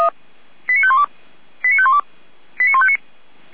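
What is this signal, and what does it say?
Five-tone selective call (ZVEI), heard as quick runs of five tones, tones of about 70 ms. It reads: 89512 89512 89159
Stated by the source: Five-tone ZVEI selective call received by SDR, from the feuerwehr-orlamuende.org fire brigade server.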